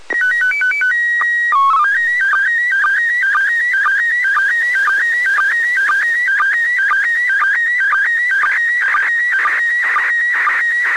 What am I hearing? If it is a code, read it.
SSTV (slow-scan television), PD 120.
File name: posts / receiving-sstv-images-from-iss / iss sstv1